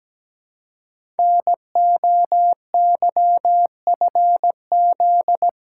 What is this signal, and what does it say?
Morse code, heard as a keyed tone, about 17 wpm, 700 Hz. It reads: NOYFZ
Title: Code Groups - 5 Letters - 17wpm